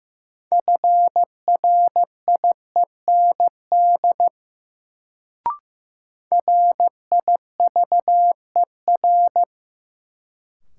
Morse code, heard as a keyed tone, about 15 wpm, 700 Hz. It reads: FRIEND RIVER
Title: Sentences from Top 300 Words - New Word Drill - 15wpm